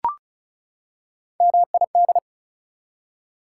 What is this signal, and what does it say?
Morse code, heard as a keyed tone, about 36 wpm, 700 Hz. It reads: MID
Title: Prefixes and Suffixes 35wpm